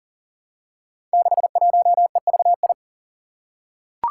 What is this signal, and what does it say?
Morse code, heard as a keyed tone, about 40 wpm, 700 Hz. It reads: 61EVI
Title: Five Challenging Letter-Number Combinations 40wpm